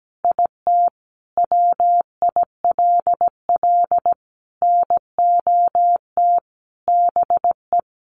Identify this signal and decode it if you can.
Morse code, heard as a keyed tone, about 17 wpm, 700 Hz. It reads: IT WILL NOT BE
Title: Sentences from Top 800 Words 17wpm